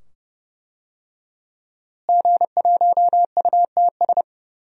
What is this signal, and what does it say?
Morse code, heard as a keyed tone, about 31 wpm, 700 Hz. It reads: G1UTS
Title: DXCC-100 Call Signs - 5 Characters - Repeated 2x - 30wpm